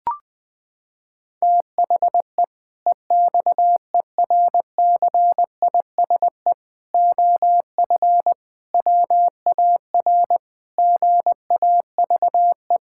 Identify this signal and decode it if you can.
Morse code, heard as a keyed tone, about 20 wpm, 700 Hz. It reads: THE EXERCISE OF WAR GAVE
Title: Sentences from Top 1000 Words - Review Part 6 - 20wpm